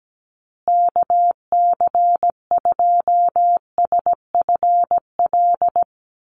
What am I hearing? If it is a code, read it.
Morse code, heard as a keyed tone, about 17 wpm, 700 Hz. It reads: KC2SFL
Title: USA Call Signs Repeated 2x 17wpm